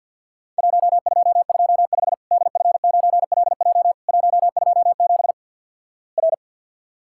Morse code, heard as a keyed tone, about 50 wpm, 700 Hz. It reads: 1225 DF9LJ 127 R
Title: Worked All Europe DX Contest - QTC Exchange with Repeats - 50wpm